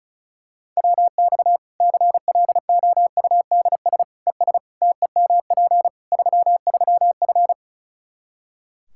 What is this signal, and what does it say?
Morse code, heard as a keyed tone, about 36 wpm, 700 Hz. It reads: WX CLOUDS ES TEMP 33F
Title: K1USN SST QSO Practice 35wpm